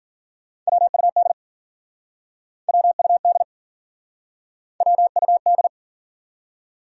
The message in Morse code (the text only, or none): WUD WUD WUD